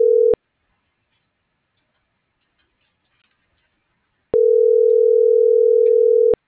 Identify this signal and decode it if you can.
Ringback tone (North America), 440 + 480 Hz.